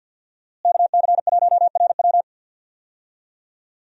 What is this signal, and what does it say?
Morse code, heard as a keyed tone, about 53 wpm, 700 Hz. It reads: KC1RW
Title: USA Call Signs - 5 Characters - Repeated 2x - 50wpm